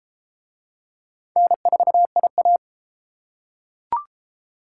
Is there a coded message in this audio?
Yes: Morse code from 1.4 to 2.6 s, heard as a keyed tone, about 33 wpm, 700 Hz. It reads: N4IA